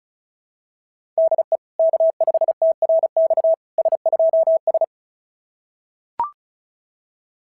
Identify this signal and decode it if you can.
Morse code, heard as a keyed tone, about 36 wpm, 660 Hz. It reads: DE K5TRX S2S
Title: SOTA QSO Practice 35wpm